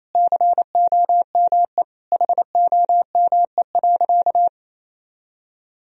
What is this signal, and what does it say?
Morse code, heard as a keyed tone, about 28 wpm, 700 Hz. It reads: COME HOME.
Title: Sentences from Top 1000 Words - Review Part 3 - 28wpm